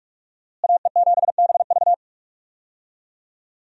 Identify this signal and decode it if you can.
Morse code, heard as a keyed tone, about 47 wpm, 700 Hz. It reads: AE7BV
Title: USA Call Signs - 5 Characters - Repeated 3x - 45wpm